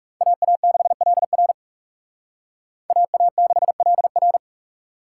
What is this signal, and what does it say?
Morse code, heard as a keyed tone, about 38 wpm, 700 Hz. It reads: AA6LR AA6LR